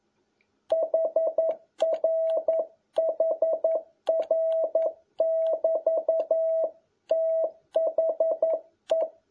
Morse code, heard as a keyed tone, about 11 wpm, 650 Hz. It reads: HRHR<BT>THE
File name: jh8cbh / 2021cw